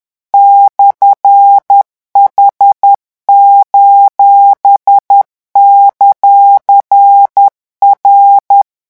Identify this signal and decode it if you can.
Morse code, heard as a keyed tone, about 10 wpm, 800 Hz. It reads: /H:;R
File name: morse mp3 / revision19-vitesse 10 mots